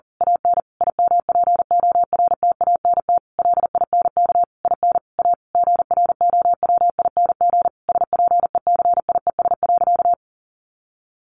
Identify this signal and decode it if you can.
Morse code, heard as a keyed tone, about 40 wpm, 700 Hz. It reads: AN IMPORTANT LINK IN A GROWING SPECIES.
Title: A Sound Of Thunder 40wpm